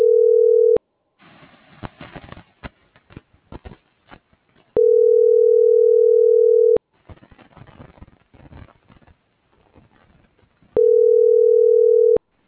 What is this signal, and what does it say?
Ringback tone (North America), 440 + 480 Hz.